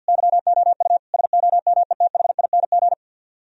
Morse code, heard as a keyed tone, about 64 wpm, 700 Hz. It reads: YOU SOMETHING